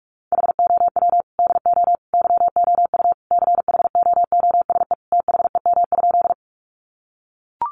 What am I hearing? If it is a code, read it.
Morse code, heard as a keyed tone, about 44 wpm, 700 Hz. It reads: HOW DO YOU CHOOSE THEM?
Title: Sentences from Top 700 Words 45wpm